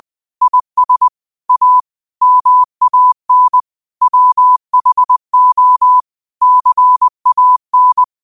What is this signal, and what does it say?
Morse code, heard as a keyed tone, about 20 wpm, 1000 Hz. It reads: IS A MAN WHO CAN